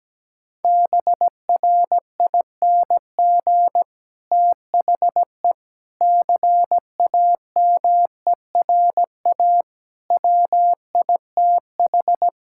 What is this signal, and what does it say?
Morse code, heard as a keyed tone, about 17 wpm, 700 Hz. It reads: BRING THE CAMERA WITH
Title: Sentences from Top 1000 Words - Review Part 5 - 17wpm